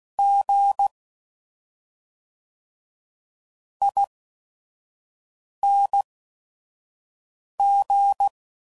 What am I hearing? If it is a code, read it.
Morse code, heard as a keyed tone, about 16 wpm, spaced as for 5 wpm, 800 Hz. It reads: G ING
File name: Suffixes / Special ING's / SlowToFaster ING's